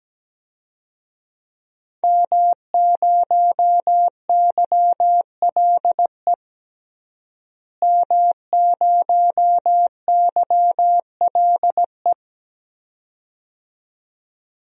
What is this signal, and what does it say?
Morse code, heard as a keyed tone, about 17 wpm, 700 Hz. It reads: M0YLE M0YLE
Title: DXCC-100 Call Signs - 5 Characters - Repeated 2x - 17wpm